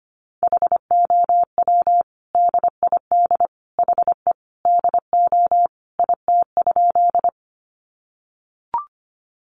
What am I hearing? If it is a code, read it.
Morse code, heard as a keyed tone, about 25 wpm, 700 Hz. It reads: HOW DID HE DO IT?
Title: Sentences from Top 100 Words 25wpm